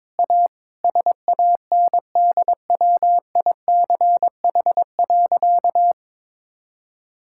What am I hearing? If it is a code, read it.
Morse code, heard as a keyed tone, about 21 wpm, 690 Hz. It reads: A SANDWICH.
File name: practice / 22wpm / Easy Sentences with Hints - 22wpm